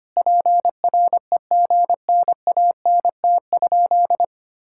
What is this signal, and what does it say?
Morse code, heard as a keyed tone, about 25 wpm, 700 Hz. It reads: PREGNANT?
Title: Sentences from Top 1000 Words - Review Part 6 - 25wpm